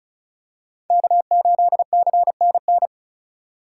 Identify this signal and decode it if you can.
Morse code, heard as a keyed tone, about 36 wpm, 700 Hz. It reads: K8CNN
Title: USA Call Signs - 5 Characters - 4x Character Spacing - 35wpm